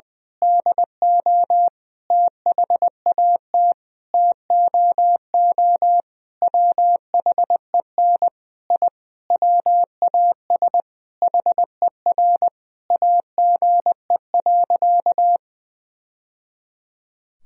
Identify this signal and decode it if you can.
Morse code, heard as a keyed tone, about 21 wpm, 700 Hz. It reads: DO THAT TOO WHEN I WAS HER AGE.
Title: Sentences from Top 1000 Words - Review Part 3 - 20wpm